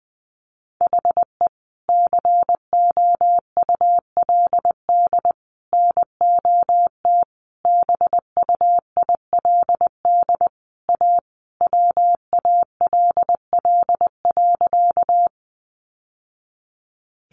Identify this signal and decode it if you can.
Morse code, heard as a keyed tone, about 20 wpm, 700 Hz. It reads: HE COULD NOT BUILD A WALL.
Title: Sentences from Top 1100 Words 20wpm